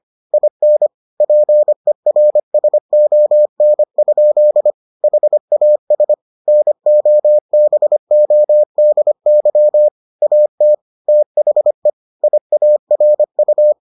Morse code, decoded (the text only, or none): IN PERSON? HAS NOBODY AT THE IARU